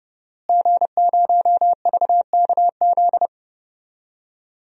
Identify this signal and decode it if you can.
Morse code, heard as a keyed tone, about 32 wpm, 700 Hz. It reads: G0VKZ
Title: DXCC-100 Call Signs - 5 Characters - Repeated 2x - 30wpm